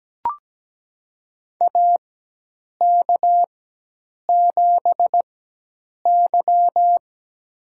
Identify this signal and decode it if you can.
Morse code, heard as a keyed tone, about 17 wpm, spaced as for 8 wpm, 700 Hz. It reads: AK7Y
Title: USA Call Signs - 4 Characters - 4x Character Spacing - 17wpm